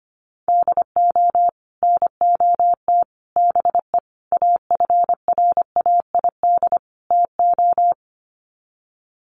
Morse code, heard as a keyed tone, about 25 wpm, 700 Hz. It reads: DO NOT BE AFRAID TO
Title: Sets of 5 Words Encore2 25wpm